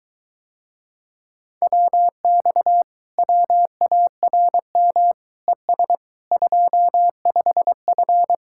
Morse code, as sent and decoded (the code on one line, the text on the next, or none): .-- -..- / .-- .- .-. -- / . ... / ..--- ..... ..-.
WX WARM ES 25F